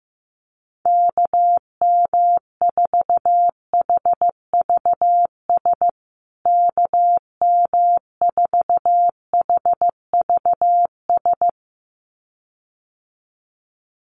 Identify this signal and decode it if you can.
Morse code, heard as a keyed tone, about 15 wpm, 700 Hz. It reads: KM4HVS KM4HVS
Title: USA Call Signs Repeated 2x 15wpm